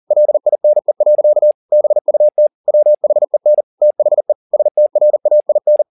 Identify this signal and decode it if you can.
Morse code, heard as a keyed tone, about 40 wpm, 600 Hz. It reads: LINE. BUT WHEN THE STRAIN
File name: the old man and the sea 2 40wpm